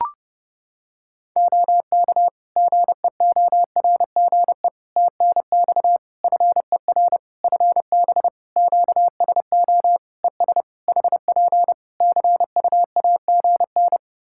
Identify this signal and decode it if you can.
Morse code, heard as a keyed tone, about 30 wpm, 700 Hz. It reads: OK GEORGE TNX FER FB QSO ES HP CUAGN
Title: K1USN SST QSO Practice 30wpm